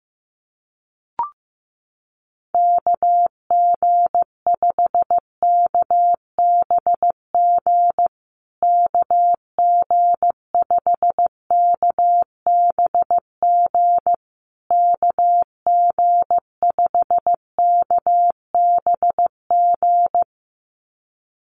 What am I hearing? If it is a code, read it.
Morse code, heard as a keyed tone, about 15 wpm, 700 Hz. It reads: KG5KBG KG5KBG KG5KBG
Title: USA Call Signs Repeated 3x 15wpm